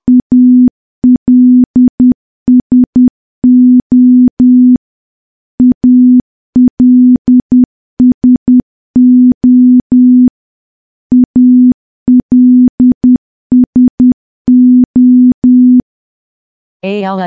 Morse code, heard as a keyed tone, about 10 wpm, 250 Hz. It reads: ALSO ALSO ALSO
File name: morse / morse-training-10WPM